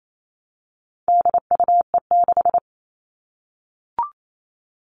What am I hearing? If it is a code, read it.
Morse code, heard as a keyed tone, about 28 wpm, 700 Hz. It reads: DUE6
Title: All Cut Numbers - 4-Digit 28wpm